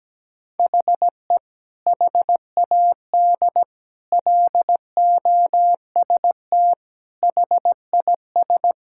Morse code, heard as a keyed tone, about 17 wpm, 700 Hz. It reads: HE HAD LOST HIS
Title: Sentences from Top 1600 Words 17wpm